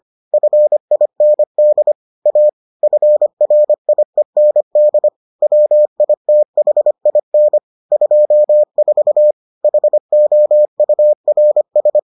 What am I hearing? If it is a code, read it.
Morse code, heard as a keyed tone, about 25 wpm, 600 Hz. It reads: FIND A FRIEND WITHIN 24 HOURS